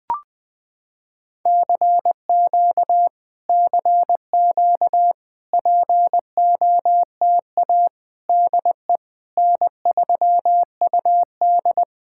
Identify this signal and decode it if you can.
Morse code, heard as a keyed tone, about 20 wpm, 700 Hz. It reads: CQ CQ POTA DE N3UD
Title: POTA QSO Practice - 20wpm